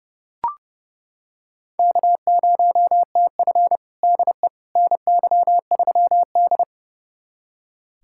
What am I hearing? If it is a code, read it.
Morse code, heard as a keyed tone, about 32 wpm, 700 Hz. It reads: K0TF DE NY3D
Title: USA Callsign DE Callsign - 4 Characters - 30wpm